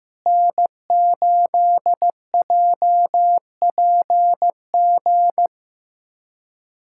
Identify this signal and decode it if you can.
Morse code, heard as a keyed tone, about 15 wpm, 700 Hz. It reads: N8JPG
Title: USA Callsign DE Callsign - 5 Characters - 15wpm